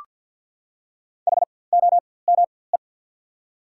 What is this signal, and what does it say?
Morse code, heard as a keyed tone, about 53 wpm, spaced as for 24 wpm, 700 Hz. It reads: HOME
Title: 4x Character Spacing - Top 500 Words - 50wpm